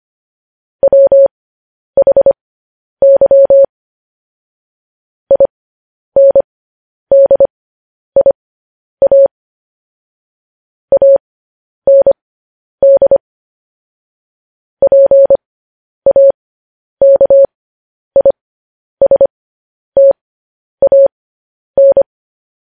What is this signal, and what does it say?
Morse code, heard as a keyed tone, about 25 wpm, spaced as for 10 wpm, 550 Hz. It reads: WHY INDIA AND PAKISTAN
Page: news-headlines-in-morse-code-at-10-wpm